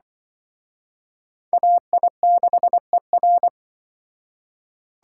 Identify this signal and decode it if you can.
Morse code, heard as a keyed tone, about 23 wpm, 700 Hz. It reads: AI6ER
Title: Speed Racing - USA Call Signs 20wpm